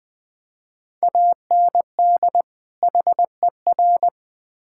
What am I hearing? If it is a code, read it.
Morse code, heard as a keyed tone, about 21 wpm, 700 Hz. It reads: AND HER
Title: Build A Phrase - 2-Word Phrases - 20wpm